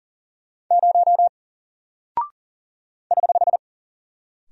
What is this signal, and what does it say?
Morse code, heard as a keyed tone, about 40 wpm, 700 Hz. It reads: 0 <HH>
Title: Single Character (Letter:Number:Punctuation:Prosign) 40wpm